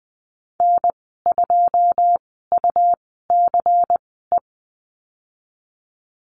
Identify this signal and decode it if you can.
Morse code, heard as a keyed tone, about 20 wpm, spaced as for 15 wpm, 700 Hz. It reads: N2UCE
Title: 122 Lesson 20g - Number 3 - Call Signs 20wpm